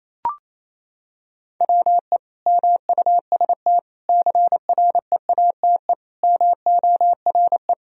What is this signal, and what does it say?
Morse code, heard as a keyed tone, about 28 wpm, 700 Hz. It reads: WE MUST CREATE MORE
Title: Sentences from Top 1000 Words - Review Part 3 - 28wpm